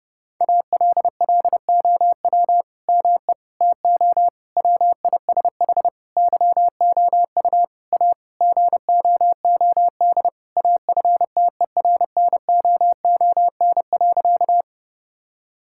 Morse code, heard as a keyed tone, about 32 wpm, 700 Hz. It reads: ALLOW ME TO WISH YOU A GOOD AFTERNOON.
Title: Sentences from Top 1600 Words 30wpm